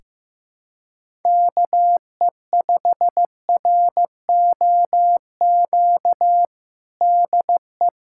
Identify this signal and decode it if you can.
Morse code, heard as a keyed tone, about 15 wpm, 700 Hz. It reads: KE5ROQ DE
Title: USA Callsign DE Callsign - 6 Characters - 15wpm